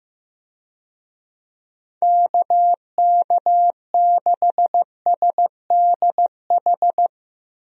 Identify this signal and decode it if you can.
Morse code, heard as a keyed tone, about 15 wpm, 700 Hz. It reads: KK6SDH
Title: USA Call Signs - 6 Characters - 15wpm